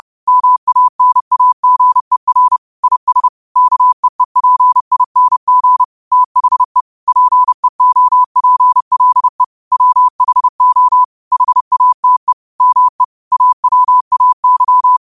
Morse code, heard as a keyed tone, about 30 wpm, 1000 Hz. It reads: MANAGER IS KEEPING THE PEOPLE WHO HATE ME AWAY
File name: qotdfiles / 30wpm